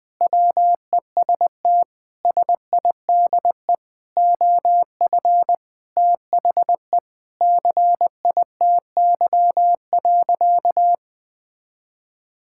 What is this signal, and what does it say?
Morse code, heard as a keyed tone, about 21 wpm, 700 Hz. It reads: WEST SIDE OF THE CITY.
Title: Sentences with 6-Word Phrases - Spoken Beginning - 20wpm